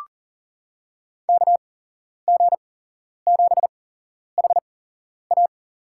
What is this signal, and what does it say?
Morse code, heard as a keyed tone, about 40 wpm, spaced as for 11 wpm, 700 Hz. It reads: KG7HA
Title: USA Call Signs - 5 Characters - 8x Character Spacing - 40wpm